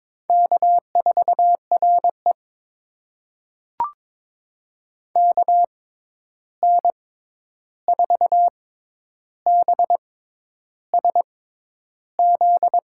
Morse code, heard as a keyed tone, about 23 wpm, spaced as for 10 wpm, 700 Hz. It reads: K4RE KN4BSZ
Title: USA Call Signs - 6x Character Spacing - 22wpm